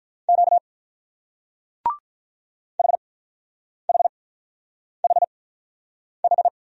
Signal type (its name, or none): Morse code